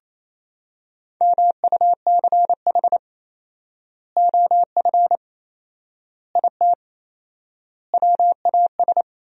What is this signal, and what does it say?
Morse code, heard as a keyed tone, about 28 wpm, 700 Hz. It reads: MUCH OF IT WAS
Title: 4x Word-Spacing - Sentences from Top 400 Words - 28wpm